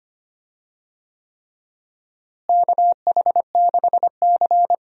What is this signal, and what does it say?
Morse code, heard as a keyed tone, about 25 wpm, 700 Hz. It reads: KH6C
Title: DXCC-100 Call Signs - 4 Characters - Repeated 3x - 25wpm